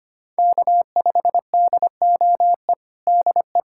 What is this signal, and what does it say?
Morse code, heard as a keyed tone, about 25 wpm, 700 Hz. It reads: K5DOE DE
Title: USA Callsign DE Callsign - 5 Characters - 25wpm